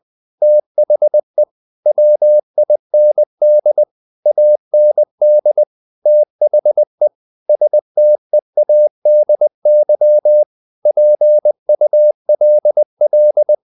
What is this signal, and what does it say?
Morse code, heard as a keyed tone, about 20 wpm, 600 Hz. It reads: THE WIND AND THE STEADY PULL
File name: the old man and the sea part 2 20wpm